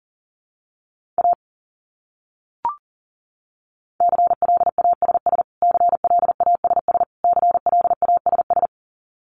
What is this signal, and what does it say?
Morse code, heard as a keyed tone, about 40 wpm, 700 Hz. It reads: A CLASS CLASS CLASS
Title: Top 500 Words Repeated 40wpm